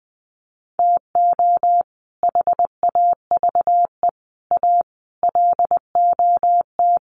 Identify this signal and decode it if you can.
Morse code, heard as a keyed tone, about 20 wpm, 700 Hz. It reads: TO HAVE A LOT
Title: Sets of 5 Words Encore 20wpm